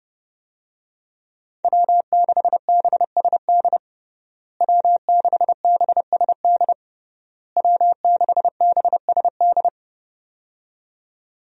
Morse code, heard as a keyed tone, about 29 wpm, 700 Hz. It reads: W6BSD W6BSD W6BSD
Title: USA Call Signs - 5 Characters - Repeated 3x - 30wpm